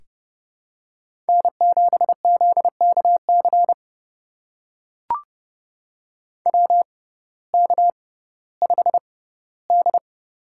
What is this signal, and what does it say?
Morse code, heard as a keyed tone, about 29 wpm, spaced as for 16 wpm, 700 Hz. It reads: N7ZKC WK5D
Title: USA Call Signs - 6x Character Spacing - 30wpm